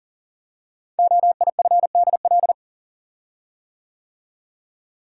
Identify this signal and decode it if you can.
Morse code, heard as a keyed tone, about 40 wpm, 700 Hz. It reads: OIFDL
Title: Code Groups - 5 Letters - 40wpm